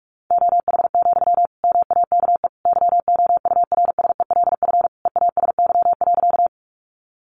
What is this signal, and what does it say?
Morse code, heard as a keyed tone, about 44 wpm, 700 Hz. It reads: OH, MAKE YOURSELF EASY.
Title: Sentences from Top 700 Words 45wpm